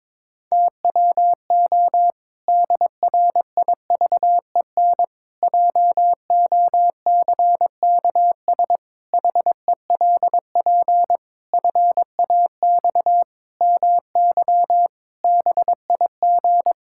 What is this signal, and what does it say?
Morse code, heard as a keyed tone, about 22 wpm, 700 Hz. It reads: TWO DRIVEN JOCKS HELP FAX MY BIG
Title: Pangram Challenge v2 - 22wpm